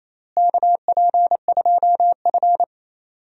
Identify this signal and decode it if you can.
Morse code, heard as a keyed tone, about 27 wpm, 700 Hz. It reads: KP2F
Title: DXCC-100 Call Signs - 4 Characters - Repeated 2x - 28wpm